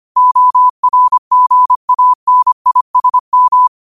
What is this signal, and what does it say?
Morse code, heard as a keyed tone, about 25 wpm, 1000 Hz. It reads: ORGANISM
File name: qotdfiles / 25wpm